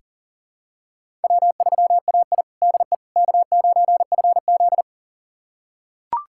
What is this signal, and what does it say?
Morse code, heard as a keyed tone, about 40 wpm, 690 Hz. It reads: W3AI DE K9FZ
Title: USA Callsign DE Callsign - 4 Characters - 40wpm